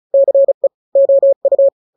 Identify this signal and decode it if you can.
Morse code, heard as a keyed tone, about 34 wpm, 550 Hz. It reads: CE OU
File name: n-grams / 2-grams 35 30 WPM